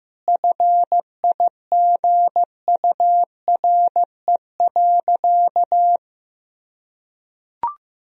Morse code, heard as a keyed tone, about 15 wpm, 700 Hz. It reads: FIGURE.
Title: Sentences from Top 1000 Words - Review Part 2 - 15wpm